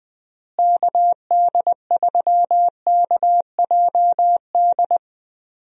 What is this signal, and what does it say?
Morse code, heard as a keyed tone, about 21 wpm, 700 Hz. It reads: KD3KJD
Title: USA Call Signs - 6 Characters - Repeated 2x - 20wpm